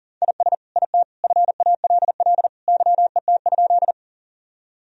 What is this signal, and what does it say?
Morse code, heard as a keyed tone, about 40 wpm, 700 Hz. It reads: IS IT FALL YET?
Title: Sentences from Top 1000 Words - Review Part 2 - 40wpm